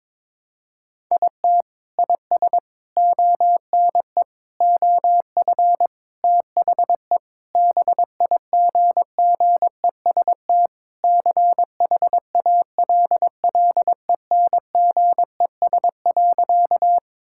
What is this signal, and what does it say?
Morse code, heard as a keyed tone, about 22 wpm, 700 Hz. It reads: IT IS ONE OF THE BIGGEST CHALLENGES.
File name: practice / 22wpm / Sentences with 4-Word Phrases - 22wpm